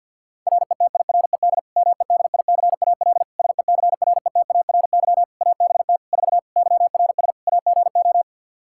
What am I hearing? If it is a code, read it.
Morse code, heard as a keyed tone, about 51 wpm, 700 Hz. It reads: RETIRED MEDICAL SECRETARY ABT 4 YRS AGO
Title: Practice QSO - Exchange 3 - More Info - 50wpm